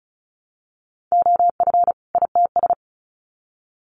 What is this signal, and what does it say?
Morse code, heard as a keyed tone, about 34 wpm, 700 Hz. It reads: OF ITS
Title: Sets of 2 Words 35wpm